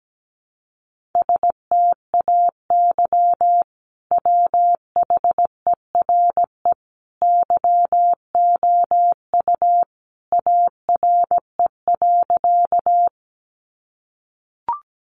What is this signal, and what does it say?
Morse code, heard as a keyed tone, about 17 wpm, 700 Hz. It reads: STAY WHERE YOU ARE.
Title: Sentences from Top 500 Words 17wpm